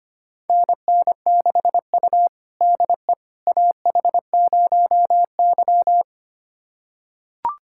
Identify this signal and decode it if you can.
Morse code, heard as a keyed tone, about 25 wpm, 700 Hz. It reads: NN6U DE AH0Y
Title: USA Callsign DE Callsign - 4 Characters - 25wpm